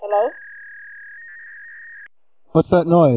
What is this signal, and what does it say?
Fax machine handshake, with speech.